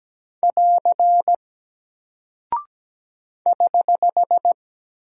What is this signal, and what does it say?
Morse code, heard as a keyed tone, about 17 wpm, 700 Hz. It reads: <AR> <HH>
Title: Single Character (Letter:Number:Punctuation:Prosign) 17wpm